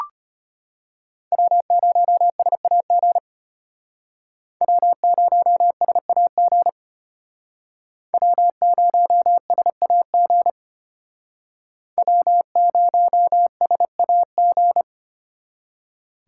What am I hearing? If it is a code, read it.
Morse code, heard as a keyed tone, about 32 wpm, 700 Hz. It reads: W0SAG W0SAG W0SAG W0SAG